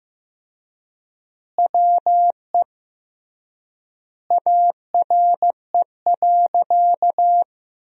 Morse code, heard as a keyed tone, about 15 wpm, 700 Hz. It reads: WE ARE.
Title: 3x Word-Spacing - Sentences from Top 200 Words - 15wpm